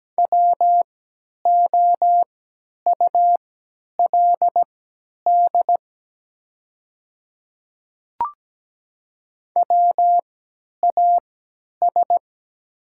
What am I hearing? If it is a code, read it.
Morse code, heard as a keyed tone, about 17 wpm, spaced as for 10 wpm, 700 Hz. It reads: WOULD WAS
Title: 3x Character Spacing - Top 100 Words - 17wpm